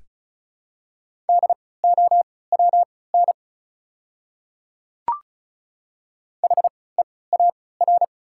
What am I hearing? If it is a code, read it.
Morse code, heard as a keyed tone, about 34 wpm, spaced as for 20 wpm, 690 Hz. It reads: DOWN HEAR
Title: 3x Character Spacing - Top 500 Words - 35wpm